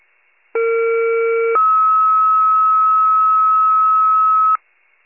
Two-tone page, heard as a one-second tone, then a three-second tone, about 460 then 1290 Hz. It is heard from 0.5 to 4.6 s.
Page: 39 RESCUE